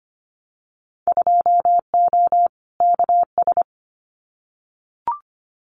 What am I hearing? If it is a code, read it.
Morse code, heard as a keyed tone, about 25 wpm, 700 Hz. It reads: 2O KS
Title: Winter Field Day Exchanges Encore 25wpm